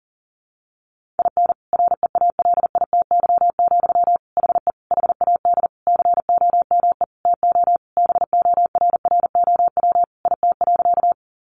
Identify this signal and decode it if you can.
Morse code, heard as a keyed tone, about 40 wpm, 700 Hz. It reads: IN REALITY, HE HAD COME TO BORROW IT.
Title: Sentences from Top 1500 Words 40wpm